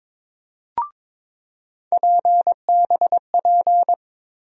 Morse code, heard as a keyed tone, about 22 wpm, 700 Hz. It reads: PBP
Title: USA Call Sign Postfixes - 3 Letters 22wpm